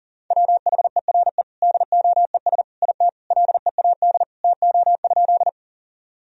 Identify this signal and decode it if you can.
Morse code, heard as a keyed tone, about 40 wpm, 700 Hz. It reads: WHERE DOES IT LEAD TO?
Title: Sentences from Top 1000 Words - Review Part 2 - 40wpm